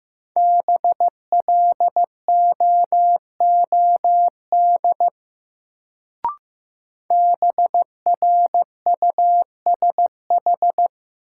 Morse code, heard as a keyed tone, about 15 wpm, 700 Hz. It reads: BLOOD BRUSH
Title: Sentences from Top 1100 Words - New Word Drill Alt - 15wpm